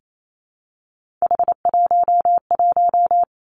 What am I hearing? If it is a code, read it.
Morse code, heard as a keyed tone, about 28 wpm, 700 Hz. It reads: H11
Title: Three Challenging Letter-Number Combinations 28wpm